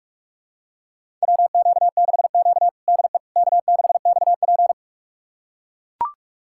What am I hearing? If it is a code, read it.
Morse code, heard as a keyed tone, about 42 wpm, 700 Hz. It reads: WQ6Q DE K6XP
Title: USA Callsign DE Callsign - 4 Characters - 45wpm